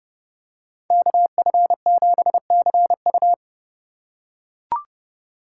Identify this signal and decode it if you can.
Morse code, heard as a keyed tone, about 30 wpm, 690 Hz. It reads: KF7CU
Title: USA Call Signs - 4x Character Spacing - 30wpm